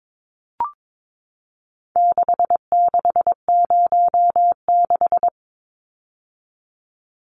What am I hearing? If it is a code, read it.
Morse code, heard as a keyed tone, about 22 wpm, 700 Hz. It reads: 6606
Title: Numbers - 4-Digit 22wpm